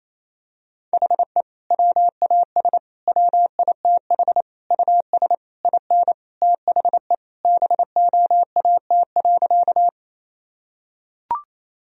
Morse code, heard as a keyed tone, about 28 wpm, 700 Hz. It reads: HE WAS WITH US IN THE BOAT.
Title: Sentences from Top 1000 Words - Review Part 3 - 28wpm